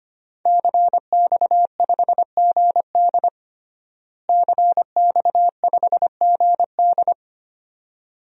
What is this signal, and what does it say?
Morse code, heard as a keyed tone, about 25 wpm, 700 Hz. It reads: CX5GD CX5GD
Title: DXCC-100 Call Signs - 5 Characters - Repeated 2x - 25wpm